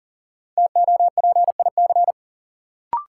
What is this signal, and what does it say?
Morse code, heard as a keyed tone, about 40 wpm, 700 Hz. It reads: TOPIC
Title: Sentences from Top 1200 Words - New Word Drill - 40wpm